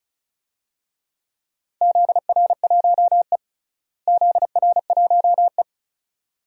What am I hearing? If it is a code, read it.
Morse code, heard as a keyed tone, about 36 wpm, 700 Hz. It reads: ZR1E ZR1E
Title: DXCC-100 Call Signs - 4 Characters - Repeated 2x - 35wpm